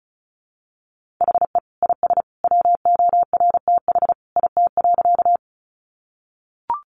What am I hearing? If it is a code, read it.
Morse code, heard as a keyed tone, about 34 wpm, 700 Hz. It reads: HE IS WORTH IT.
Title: Sentences from Top 800 Words 35wpm